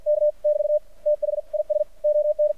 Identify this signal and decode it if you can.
Morse code, heard as a keyed tone, about 53 wpm, spaced as for 46 wpm, 600 Hz. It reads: K <BT> THIS ON
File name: archived / CRSnet